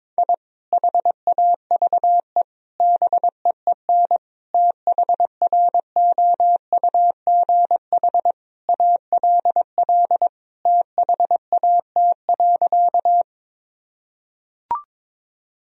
Morse code, heard as a keyed tone, about 22 wpm, 700 Hz. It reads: I HAVE BEEN THROUGH ALL THAT.
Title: Sentences from Top 1000 Words - Review Part 1 - 22wpm